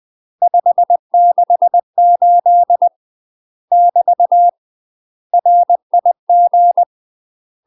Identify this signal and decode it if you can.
Morse code, heard as a keyed tone, about 20 wpm, 700 Hz. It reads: 568 <BT> RIG
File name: UK MASTER QSOs TRAINING / 20WPM / UK MASTER QSOs 0061